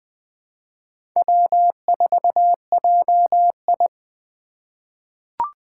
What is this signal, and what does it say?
Morse code, heard as a keyed tone, about 21 wpm, 700 Hz. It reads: W4JI
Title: USA Call Signs - 2x Character Spacing - 20wpm